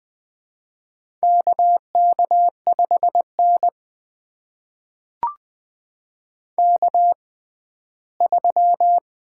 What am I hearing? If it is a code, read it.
Morse code, heard as a keyed tone, about 20 wpm, spaced as for 14 wpm, 700 Hz. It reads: KK5N K3
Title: USA Call Signs - 4 Characters - 6x Character Spacing - 20wpm